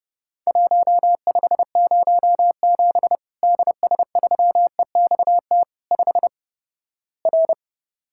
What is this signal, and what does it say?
Morse code, heard as a keyed tone, about 30 wpm, 700 Hz. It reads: 1507 DS3EXT 5 R
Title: Worked All Europe DX Contest - QTC Exchange - 30wpm